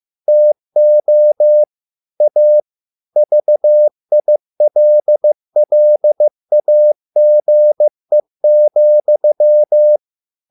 Morse code, heard as a keyed tone, about 15 wpm, 600 Hz. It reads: TO A VILLAGE,